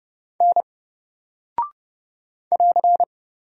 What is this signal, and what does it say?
Morse code, heard as a keyed tone, about 30 wpm, 700 Hz. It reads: N <AR>
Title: Single Character (Letter:Number:Punctuation:Prosign) Alt 30wpm